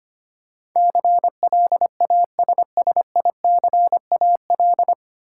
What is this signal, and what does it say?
Morse code, heard as a keyed tone, about 25 wpm, 700 Hz. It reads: CLASSICAL
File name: Nine Letter Words - 25wpm